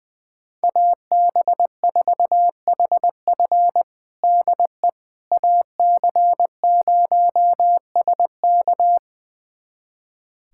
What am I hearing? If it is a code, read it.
Morse code, heard as a keyed tone, about 21 wpm, 700 Hz. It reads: AB4HF DE AC0SK
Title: USA Callsign DE Callsign - 5 Characters - 20wpm